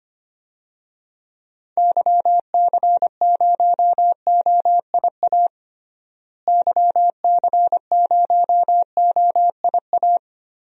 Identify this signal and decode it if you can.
Morse code, heard as a keyed tone, about 25 wpm, 700 Hz. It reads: YC0OIA YC0OIA
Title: DXCC-100 Call Signs - 6 Characters - Repeated 2x - 25wpm